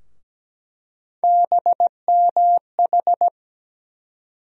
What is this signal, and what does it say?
Morse code, heard as a keyed tone, about 17 wpm, 700 Hz. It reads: BMH
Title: USA Call Sign Postfixes - 3 Letters 17wpm